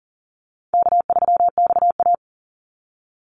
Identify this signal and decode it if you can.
Morse code, heard as a keyed tone, about 40 wpm, 700 Hz. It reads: K3XA
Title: Easy Call Signs - 40wpm